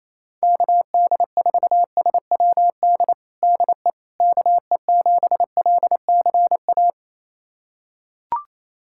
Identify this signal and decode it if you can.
Morse code, heard as a keyed tone, about 28 wpm, 700 Hz. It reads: KD4SWD DE KE7LCA
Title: USA Callsign DE Callsign - 6 Characters - 28wpm